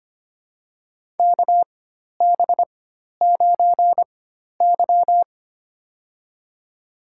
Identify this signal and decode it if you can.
Morse code, heard as a keyed tone, about 25 wpm, spaced as for 12 wpm, 700 Hz. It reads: KB9Y
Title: USA Call Signs - 4 Characters - 4x Character Spacing - 25wpm